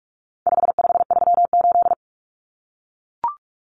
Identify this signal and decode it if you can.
Morse code, heard as a keyed tone, about 44 wpm, 700 Hz. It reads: 5538
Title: Common Cut Numbers - 4-Digit 45wpm